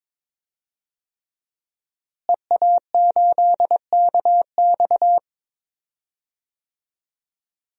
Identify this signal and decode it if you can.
Morse code, heard as a keyed tone, about 21 wpm, 700 Hz. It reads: EA8KX
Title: DXCC-100 Call Signs - 5 Characters - Repeated 3x - 22wpm